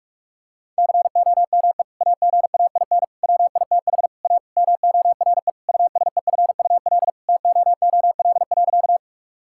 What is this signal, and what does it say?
Morse code, heard as a keyed tone, about 41 wpm, 700 Hz. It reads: COME AGAIN WITH A MORE USEFUL TOOL.
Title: Sentences from Top 1000 Words - Review Part 6 - 45wpm